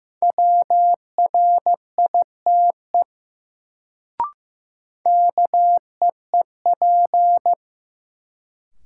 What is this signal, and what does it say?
Morse code, heard as a keyed tone, about 15 wpm, 700 Hz. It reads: WRITE KEEP
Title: Sentences from Top 300 Words - New Word Drill - 15wpm